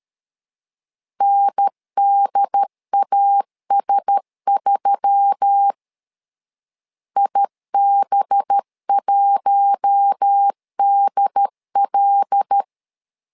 Morse code, heard as a keyed tone, about 13 wpm, 800 Hz. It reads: NDAS3 IB1DL
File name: LEZIONI / 5 LBVC / 5 LB 12wpm 1906